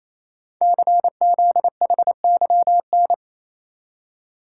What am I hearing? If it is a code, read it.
Morse code, heard as a keyed tone, about 27 wpm, 700 Hz. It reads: CZHYN
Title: Code Groups - 5 Letters - 28wpm